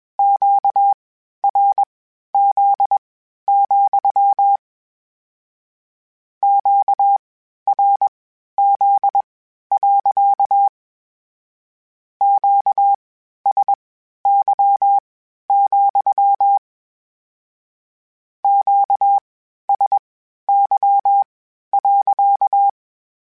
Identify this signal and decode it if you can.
Morse code, heard as a keyed tone, about 21 wpm, spaced as for 12 wpm, 800 Hz. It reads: QRZ, QRZ. QSY, QSY.
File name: Q Calls / Q Calls-mp3s / Q Calls sppc @20